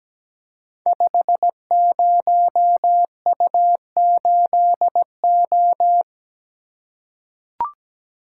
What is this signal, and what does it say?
Morse code, heard as a keyed tone, about 17 wpm, 700 Hz. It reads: 50U8O
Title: Code Groups - 5 Letter:Numbers - 17wpm